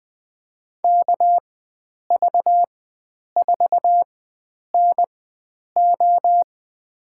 Morse code, heard as a keyed tone, about 20 wpm, spaced as for 9 wpm, 700 Hz. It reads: KV4NO
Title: USA Call Signs - 5 Characters - 4x Character Spacing - 20wpm